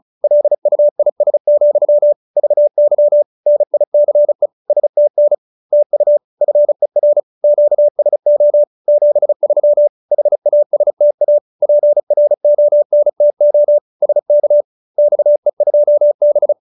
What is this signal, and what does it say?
Morse code, heard as a keyed tone, about 34 wpm, 590 Hz. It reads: LUIS, VY NICE STN TU FER QSO 73 HASTA PRONTO SK XE2B
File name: QSO304 35